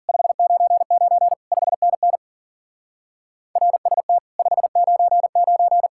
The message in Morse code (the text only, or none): 599 5NN RST 599